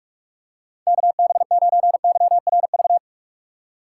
Morse code, heard as a keyed tone, about 46 wpm, 700 Hz. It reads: KB9YRV
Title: USA Call Signs - 3x Character Spacing - 45wpm